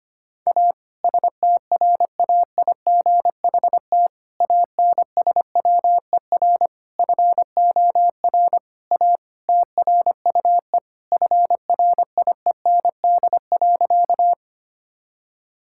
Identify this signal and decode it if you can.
Morse code, heard as a keyed tone, about 25 wpm, 700 Hz. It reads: A STRAIGHT ANSWER FOR A TRUE FRIEND.
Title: Sentences from Top 1000 Words - Review Part 6 - 25wpm